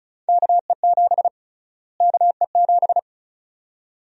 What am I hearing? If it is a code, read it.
Morse code, heard as a keyed tone, about 36 wpm, 700 Hz. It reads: KE7 KE7
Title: USA Call Sign Prefixes - Repeated 2x - 35wpm